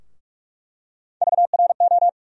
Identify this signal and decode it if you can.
Morse code, heard as a keyed tone, about 46 wpm, 700 Hz. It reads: VRO
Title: USA Call Sign Postfixes - 3 Letters 45wpm